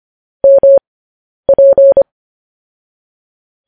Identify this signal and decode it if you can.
Morse code, heard as a keyed tone, about 25 wpm, spaced as for 10 wpm, 550 Hz. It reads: MP
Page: news headlines in morse code at 10 wpm